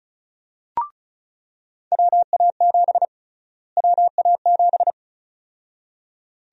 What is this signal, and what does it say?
Morse code, heard as a keyed tone, about 36 wpm, 700 Hz. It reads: WA7 WA7
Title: USA Call Sign Prefixes - Repeated 2x - 35wpm